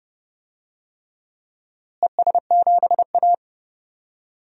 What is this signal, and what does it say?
Morse code, heard as a keyed tone, about 29 wpm, 700 Hz. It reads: ES7A
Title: DXCC-100 Call Signs - 4 Characters - Repeated 3x - 30wpm